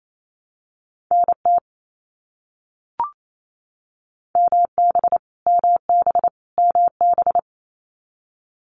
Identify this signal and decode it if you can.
Morse code, heard as a keyed tone, about 28 wpm, 700 Hz. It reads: NT MB MB MB
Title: Canadian Provinces and Territories - Repeated 3x - 28wpm